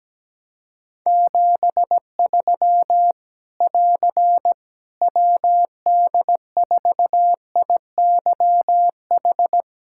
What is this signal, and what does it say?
Morse code, heard as a keyed tone, about 17 wpm, 700 Hz. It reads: 73 <AR> WD4IYH
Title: K1USN SST QSO Practice 17wpm